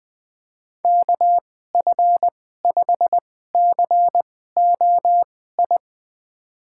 Morse code, heard as a keyed tone, about 20 wpm, spaced as for 14 wpm, 700 Hz. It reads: KF5COI
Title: USA Call Signs - 6 Characters - 2x Character Spacing - 20wpm